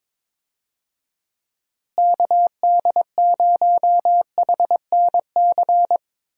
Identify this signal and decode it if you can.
Morse code, heard as a keyed tone, about 22 wpm, 700 Hz. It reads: KD0HNC
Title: USA Call Signs - 6 Characters - Repeated 2x - 22wpm